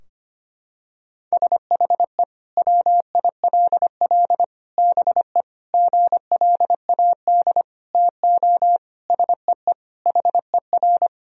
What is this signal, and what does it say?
Morse code, heard as a keyed tone, about 25 wpm, 700 Hz. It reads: SHE WILL BE GLAD TO SEE HER
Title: Sentences from Top 1000 Words - Review Part 2 - 25wpm